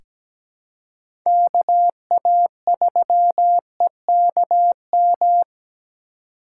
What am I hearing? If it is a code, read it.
Morse code, heard as a keyed tone, about 17 wpm, 700 Hz. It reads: KA3EKM
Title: USA Call Signs - 6 Characters - 4x Character Spacing - 17wpm